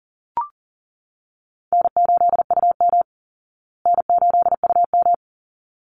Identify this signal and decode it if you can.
Morse code, heard as a keyed tone, about 40 wpm, 700 Hz. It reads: N8UM N8UM
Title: Call Signs Repeated 3x - 40wpm